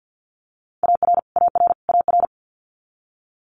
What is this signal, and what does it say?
Morse code, heard as a keyed tone, about 50 wpm, 700 Hz. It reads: AR AR AR
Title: Multi-letter Phonemes Repeated 3x 50wpm